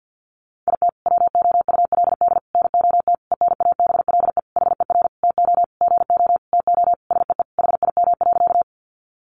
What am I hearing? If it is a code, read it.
Morse code, heard as a keyed tone, about 50 wpm, 700 Hz. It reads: IT WOULD NOT ENABLE HER TO GO TO SEE HIM.
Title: Sentences from Top 1500 Words 50wpm